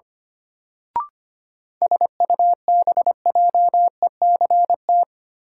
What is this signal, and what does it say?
Morse code, heard as a keyed tone, about 25 wpm, 700 Hz. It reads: SUBJECT
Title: Sentences from Top 800 Words - New Word Drill - 25wpm